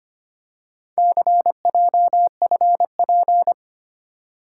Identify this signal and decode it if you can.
Morse code, heard as a keyed tone, about 25 wpm, 700 Hz. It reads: CJFP
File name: practice / 25wpm / Four Letter-Number Combinations - 25wpm